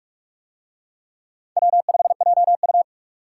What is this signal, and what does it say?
Morse code, heard as a keyed tone, about 44 wpm, 700 Hz. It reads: W5JU